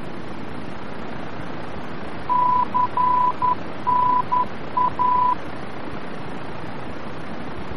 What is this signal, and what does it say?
Morse code, heard as a keyed tone, about 11 wpm, 1000 Hz, with noise. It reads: CNA
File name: france / mx / CNA-VOR-F